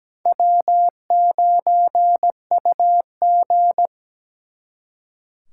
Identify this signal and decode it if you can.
Morse code, heard as a keyed tone, about 20 wpm, 700 Hz. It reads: W9UG
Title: Speed Racing - USA Call Signs 17wpm